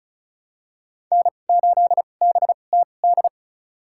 Morse code, heard as a keyed tone, about 36 wpm, spaced as for 25 wpm, 700 Hz. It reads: N8BTD
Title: USA Call Signs - 5 Characters - 2x Character Spacing - 35wpm